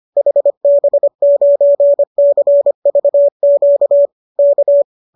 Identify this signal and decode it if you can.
Morse code, heard as a keyed tone, about 25 wpm, 570 Hz. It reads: HB9CVQ K
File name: qso210 25